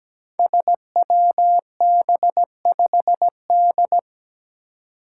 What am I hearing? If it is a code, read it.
Morse code, heard as a keyed tone, about 17 wpm, 700 Hz. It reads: SWB5D
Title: Five Challenging Letter-Number Combinations 17wpm